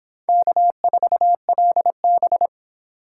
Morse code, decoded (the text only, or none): K4LB